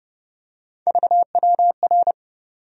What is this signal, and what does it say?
Morse code, heard as a keyed tone, about 30 wpm, 700 Hz. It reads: VWR
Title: USA Call Sign Postfixes - 3 Letters 30wpm